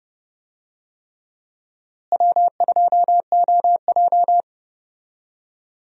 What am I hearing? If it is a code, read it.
Morse code, heard as a keyed tone, about 30 wpm, 700 Hz. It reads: W2OJ
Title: USA Call Signs - 4 Characters - Repeated 3x - 30wpm